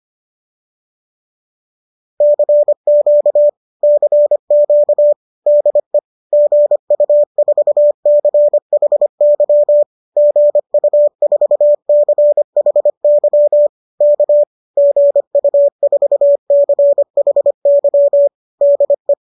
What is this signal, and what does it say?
Morse code, heard as a keyed tone, about 25 wpm, 570 Hz. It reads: CQ CQ DE GU4CHY GU4CHY K GU4CHY DE